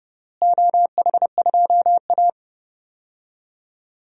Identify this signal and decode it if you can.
Morse code, heard as a keyed tone, about 30 wpm, 700 Hz. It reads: OH2A
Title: DXCC-100 Call Signs - 4 Characters - Repeated 3x - 30wpm